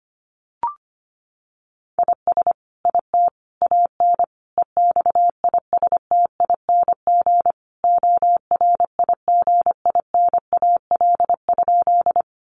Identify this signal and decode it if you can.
Morse code, heard as a keyed tone, about 25 wpm, 700 Hz. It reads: IS IT AN EXISTING ORIGINAL?
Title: Sentences from Top 1500 Words 25wpm